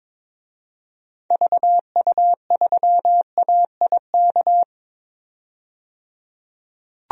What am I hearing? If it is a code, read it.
Morse code, heard as a keyed tone, about 22 wpm, 700 Hz. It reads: VU3AIK
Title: DXCC-100 Call Signs - 6 Characters - 22wpm